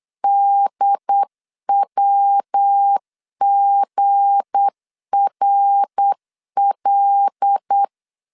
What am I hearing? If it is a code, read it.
Morse code, heard as a keyed tone, about 8 wpm, 800 Hz. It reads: DWGRL
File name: LEZIONI / 9 286 8wpm 3425